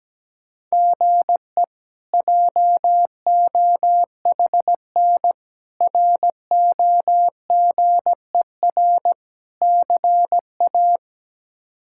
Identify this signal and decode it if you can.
Morse code, heard as a keyed tone, about 17 wpm, 700 Hz. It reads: GE JOHN ROGER CA
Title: K1USN SST QSO Practice - 17wpm